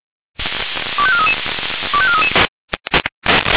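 Five-tone selective call (ZVEI), heard as quick runs of five tones, tones of about 70 ms. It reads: 25433 25433
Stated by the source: ZVEI five-tone selective call received on SDR.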